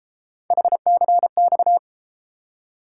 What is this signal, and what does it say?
Morse code, heard as a keyed tone, about 34 wpm, 700 Hz. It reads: HCX